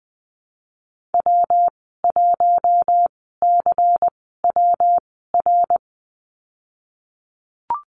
Morse code, heard as a keyed tone, about 19 wpm, spaced as for 14 wpm, 700 Hz. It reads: W1CWR
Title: 147 Lesson 24g - Letter y - Call Signs 20wpm